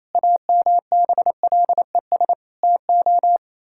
Morse code, heard as a keyed tone, about 28 wpm, 700 Hz. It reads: AMBLES TO
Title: Pangram Challenge v2 - 28wpm